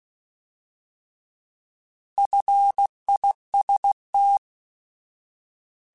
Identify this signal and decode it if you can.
Morse code, heard as a keyed tone, about 16 wpm, 800 Hz. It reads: FIST
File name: Alphabetical / Alphabetical mp3s / Fs @15